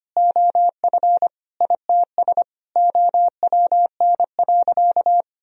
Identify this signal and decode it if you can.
Morse code, heard as a keyed tone, about 25 wpm, 700 Hz. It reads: OF ITS OWN.
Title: Sentences from Top 1000 Words - Review Part 4 - 25wpm